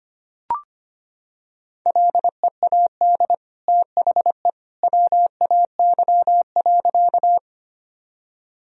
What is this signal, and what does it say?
Morse code, heard as a keyed tone, about 25 wpm, 700 Hz. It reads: LEAD THE WAY.
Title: Sentences from Top 1000 Words - Review Part 2 - 25wpm